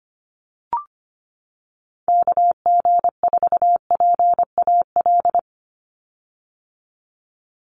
Morse code, heard as a keyed tone, about 25 wpm, 700 Hz. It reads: KG4PAL
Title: USA Call Signs 25wpm